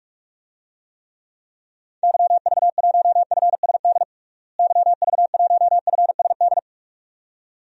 Morse code, heard as a keyed tone, about 42 wpm, 700 Hz. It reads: YV1FSD YV1FSD
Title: DXCC-100 Call Signs - 6 Characters - Repeated 2x - 45wpm